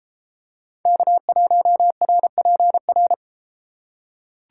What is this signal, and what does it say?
Morse code, heard as a keyed tone, about 33 wpm, 700 Hz. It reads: K1RPR